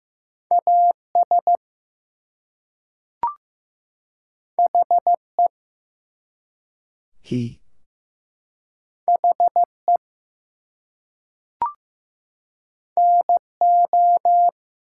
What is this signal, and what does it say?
Morse code, heard as a keyed tone, about 15 wpm, 700 Hz. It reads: AS HE HE NO
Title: Two Letter Words - 15wpm